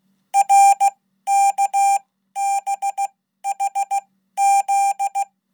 Morse code, heard as a keyed tone, about 16 wpm, spaced as for 12 wpm, 800 Hz. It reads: RKBHZ
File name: Code